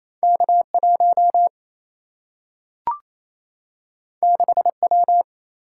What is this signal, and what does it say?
Morse code, heard as a keyed tone, about 28 wpm, 700 Hz. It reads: K 1 6 W
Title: Single-Letter Single-Number 28wpm